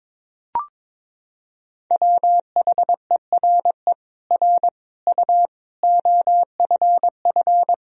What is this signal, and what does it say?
Morse code, heard as a keyed tone, about 22 wpm, 700 Hz. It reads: WHERE R U OFF
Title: Sentences from Top 200 Words and Abbreviations 22wpm